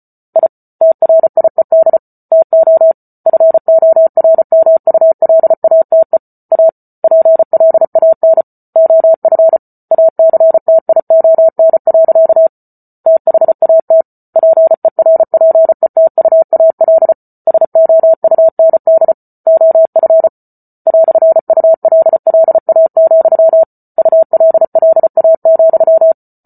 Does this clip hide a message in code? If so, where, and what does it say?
Yes: Morse code from 0.4 to 26.1 s, heard as a keyed tone, about 34 wpm, 650 Hz. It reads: I TRIED TO FORMULATE A PLAN OF ACTION. THAT PERPETUAL SOUND OF "ULLA, ULLA,